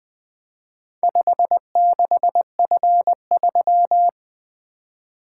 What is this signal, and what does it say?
Morse code, heard as a keyed tone, about 20 wpm, 700 Hz. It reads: 56F3
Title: Four Letter-Number Combinations - 20wpm